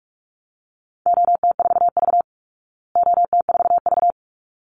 Morse code, heard as a keyed tone, about 44 wpm, 700 Hz. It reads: OT4V OT4V
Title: Call Signs Repeated 3x - 45wpm